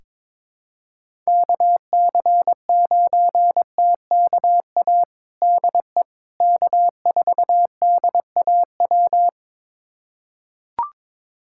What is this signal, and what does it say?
Morse code, heard as a keyed tone, about 21 wpm, 700 Hz. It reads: KC9TKA DE K4DAW
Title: USA Callsign DE Callsign 22wpm